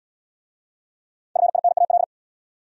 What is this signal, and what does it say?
Morse code, heard as a keyed tone, about 71 wpm, 700 Hz. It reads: FINAL